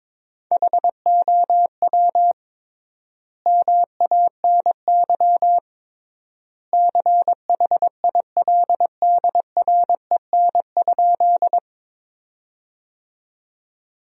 Morse code, heard as a keyed tone, about 22 wpm, 700 Hz. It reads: HOW MANY CHILDREN?
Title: 3x Word-Spacing - Sentences from Top 300 Words - 22wpm